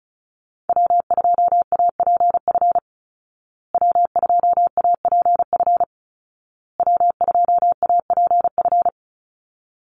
Morse code, heard as a keyed tone, about 34 wpm, 700 Hz. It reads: W2APF W2APF W2APF
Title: Call Signs Repeated 2x - 35wpm